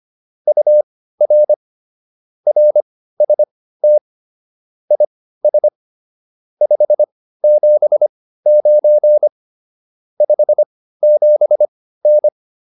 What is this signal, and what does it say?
Morse code, heard as a keyed tone, about 25 wpm, spaced as for 15 wpm, 600 Hz. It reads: UR RST IS 579 57N
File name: QSO301 15